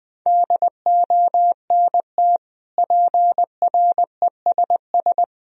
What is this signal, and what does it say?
Morse code, heard as a keyed tone, about 20 wpm, 700 Hz. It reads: DONT PRESS
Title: Sentences from Top 1000 Words - Review Part 1 - 20wpm